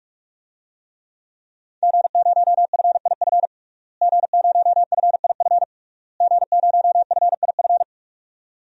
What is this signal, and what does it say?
Morse code, heard as a keyed tone, about 46 wpm, 700 Hz. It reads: G0FIF G0FIF G0FIF
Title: DXCC-100 Call Signs - 5 Characters - Repeated 3x - 45wpm